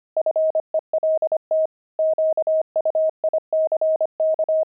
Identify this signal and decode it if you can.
Morse code, heard as a keyed tone, about 25 wpm, 630 Hz. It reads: FELT QUICK
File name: pangram-fs-25-cs-25-weaving